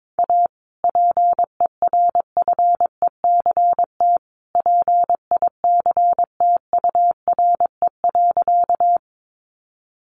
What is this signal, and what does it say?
Morse code, heard as a keyed tone, about 22 wpm, 700 Hz. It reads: A PERFECT PICTURE.
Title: Sentences from Top 1000 Words 22wpm